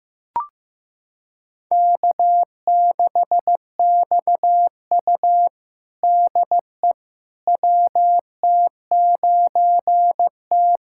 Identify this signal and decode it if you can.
Morse code, heard as a keyed tone, about 15 wpm, 700 Hz. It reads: K6XU DE WT9T
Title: USA Callsign DE Callsign - 4 Characters - 15wpm